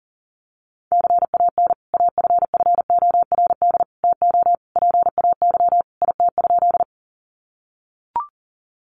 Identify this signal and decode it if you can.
Morse code, heard as a keyed tone, about 40 wpm, 700 Hz. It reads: CAN AFFORD TO PAY IT?
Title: Sentences from Top 1100 Words 40wpm